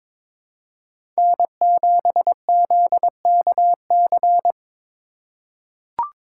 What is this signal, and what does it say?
Morse code, heard as a keyed tone, about 22 wpm, 700 Hz. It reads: N7ZKC
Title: USA Call Signs - 5 Characters - 3x Character Spacing - 22wpm